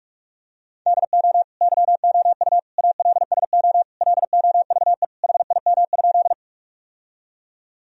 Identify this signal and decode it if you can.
Morse code, heard as a keyed tone, about 46 wpm, 700 Hz. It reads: DO YOU ALSO LOVE HIM?
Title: Sentences from Top 1000 Words - Review Part 1 - 45wpm